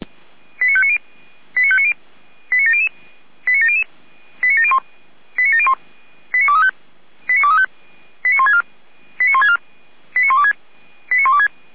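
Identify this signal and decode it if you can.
Five-tone selective call (ZVEI), heard as quick runs of five tones, tones of about 70 ms. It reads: 89590 89590 89700 89700 89701 89701 89235 89164 89164 89126 89126
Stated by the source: ZVEI five-tone selective call 89120, received on an SDR.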